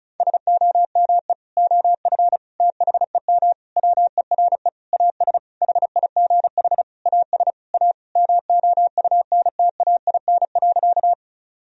Morse code, heard as a keyed tone, about 36 wpm, 700 Hz. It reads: SOME OF THEM WERE AS HIGH AS A MOUNTAIN.
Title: Sentences from Top 1000 Words - Review Part 1 - 35wpm